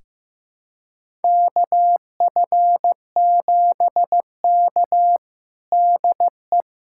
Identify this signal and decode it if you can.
Morse code, heard as a keyed tone, about 15 wpm, 700 Hz. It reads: KF7K DE
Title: USA Callsign DE Callsign - 4 Characters - 15wpm